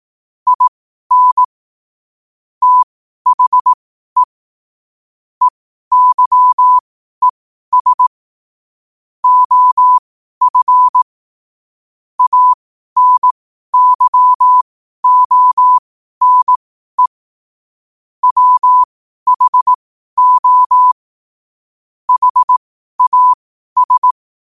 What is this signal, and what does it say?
Morse code, heard as a keyed tone, about 18 wpm, spaced as for 12 wpm, 1000 Hz. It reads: IN THE EYES OF ANYONE WHO HAS
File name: qotdfiles / 12wpm